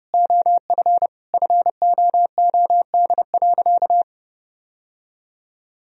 Morse code, heard as a keyed tone, about 30 wpm, 700 Hz. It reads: OF FOOD.
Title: Sentences from Top 1000 Words - Review Part 3 - 30wpm